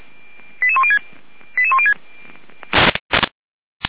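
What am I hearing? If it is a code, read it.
Five-tone selective call (ZVEI), heard as quick runs of five tones, tones of about 70 ms. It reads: 88106 88106
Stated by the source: ZVEI five-tone selective call received on an SDR.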